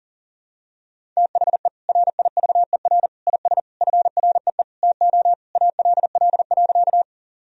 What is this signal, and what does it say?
Morse code, heard as a keyed tone, about 40 wpm, 700 Hz. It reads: THE RIVER IS FREE TO ALL.
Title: Sentences from Top 1000 Words - Review Part 1 - 40wpm